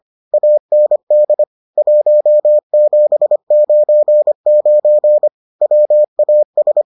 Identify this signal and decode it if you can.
Morse code, heard as a keyed tone, about 25 wpm, 600 Hz. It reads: AND 1799 WAS